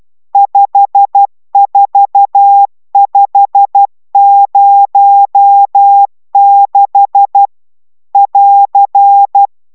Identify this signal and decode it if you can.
Morse code, heard as a keyed tone, about 12 wpm, 800 Hz. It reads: 54506 <AR>
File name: taviro tanulas / cw 69 20 10